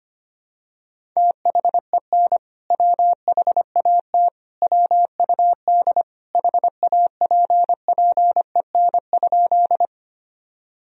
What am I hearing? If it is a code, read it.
Morse code, heard as a keyed tone, about 25 wpm, 700 Hz. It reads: THEN WHAT WUD HAPPEN?
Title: Sentences from Top 400 Words and Abbreviations 25wpm